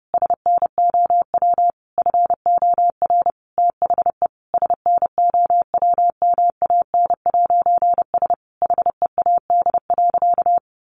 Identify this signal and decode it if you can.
Morse code, heard as a keyed tone, about 30 wpm, 700 Hz. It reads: SNOW FOR THE SNOWMAN'S HEAD.
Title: Easy Phrases 30wpm